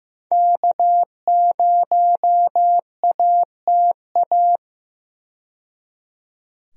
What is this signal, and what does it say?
Morse code, heard as a keyed tone, about 15 wpm, 700 Hz. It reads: K0ATA
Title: USA Call Signs - 5 Characters - 15wpm